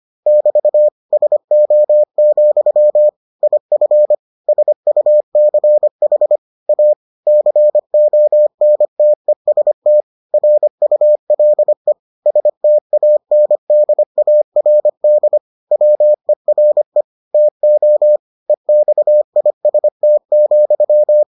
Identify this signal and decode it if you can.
Morse code, heard as a keyed tone, about 25 wpm, 600 Hz. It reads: <BT> SO, IF SUCH A CONTEST RULE STANDARD WERE TO EXIST,